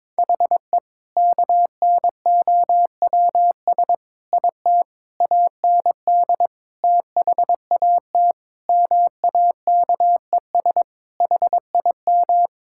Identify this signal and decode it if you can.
Morse code, heard as a keyed tone, about 22 wpm, 700 Hz. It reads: HE KNOWS IT AND THAT MAKES HIM
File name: practice / 22wpm / Sentences with 2-Word Phrases - 22wpm